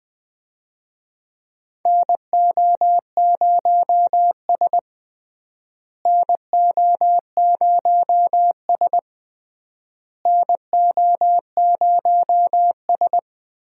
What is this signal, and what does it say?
Morse code, heard as a keyed tone, about 21 wpm, 700 Hz. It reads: NO0S NO0S NO0S
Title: USA Call Signs - 4 Characters - Repeated 3x - 20wpm